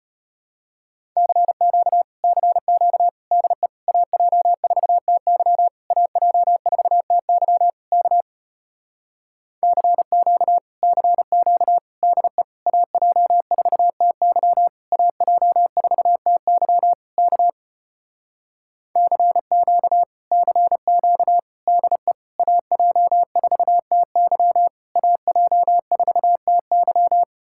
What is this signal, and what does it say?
Morse code, heard as a keyed tone, about 33 wpm, 700 Hz. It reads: CQ CQ DE AJ4TY AJ4TY K CQ CQ DE AJ4TY AJ4TY K CQ CQ DE AJ4TY AJ4TY